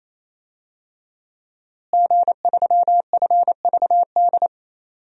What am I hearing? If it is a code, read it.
Morse code, heard as a keyed tone, about 27 wpm, 700 Hz. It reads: G3FVD
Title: DXCC-100 Call Signs - 5 Characters - Repeated 3x - 28wpm